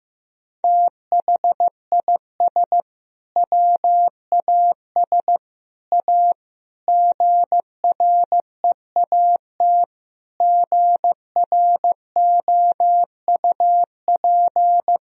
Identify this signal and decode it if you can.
Morse code, heard as a keyed tone, about 15 wpm, 700 Hz. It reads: THIS WAS A GREAT GROUP
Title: Sentences from Top 1000 Words - Review Part 1 - 15wpm